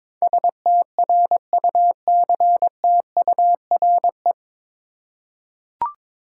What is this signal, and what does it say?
Morse code, heard as a keyed tone, about 22 wpm, 700 Hz. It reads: STRUCTURE
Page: Morse Code Ninja practice 22wpm